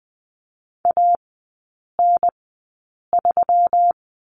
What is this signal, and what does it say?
Morse code, heard as a keyed tone, about 20 wpm, spaced as for 8 wpm, 700 Hz. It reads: AN3
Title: 178 Lesson 29e - Slash - Three Characters 20wpm